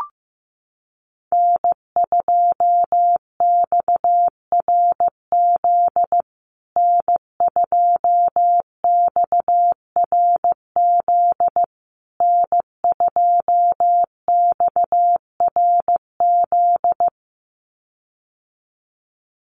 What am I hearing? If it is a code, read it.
Morse code, heard as a keyed tone, about 15 wpm, 700 Hz. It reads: N2XRZ N2XRZ N2XRZ
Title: USA Call Signs Repeated 3x 15wpm